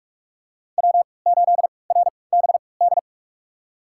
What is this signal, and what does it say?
Morse code, heard as a keyed tone, about 46 wpm, spaced as for 25 wpm, 700 Hz. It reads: W8RBD
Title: USA Call Signs - 5 Characters - 3x Character Spacing - 45wpm